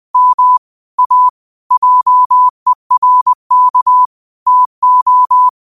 Morse code, heard as a keyed tone, about 20 wpm, 1000 Hz. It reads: M A JERK TO
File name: qotdfiles / 20wpm